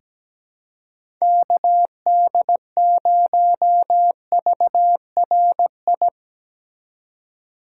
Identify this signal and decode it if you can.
Morse code, heard as a keyed tone, about 17 wpm, 700 Hz. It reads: KD0VRI
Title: USA Call Signs - 3x Character Spacing - 17wpm